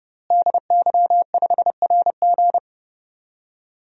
Morse code, heard as a keyed tone, about 30 wpm, 700 Hz. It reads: DY5RG
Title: DXCC-100 Call Signs - 5 Characters - Repeated 2x - 30wpm